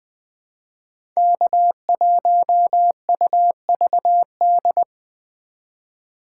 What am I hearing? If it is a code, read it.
Morse code, heard as a keyed tone, about 20 wpm, 700 Hz. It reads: K1UVD
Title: Speed Racing - USA 1x3 Call Signs 17wpm